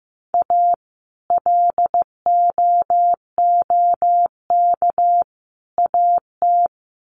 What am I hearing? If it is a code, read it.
Morse code, heard as a keyed tone, about 15 wpm, 700 Hz. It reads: A LOOK AT
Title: Sets of 3 Words 15wpm